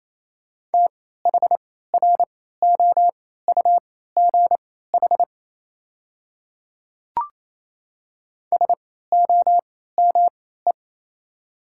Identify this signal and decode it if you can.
Morse code, heard as a keyed tone, about 28 wpm, spaced as for 16 wpm, 700 Hz. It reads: THROUGH SOME
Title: 3x Character Spacing - Top 200 Words - 28wpm